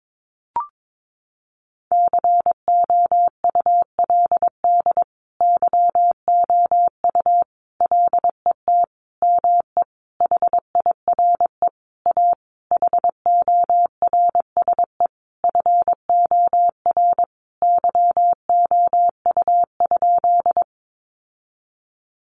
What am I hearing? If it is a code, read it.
Morse code, heard as a keyed tone, about 22 wpm, 700 Hz. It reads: COULD YOU LET ME HIRE A HORSE FOR YOU?
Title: Sentences from Top 1400 Words 22wpm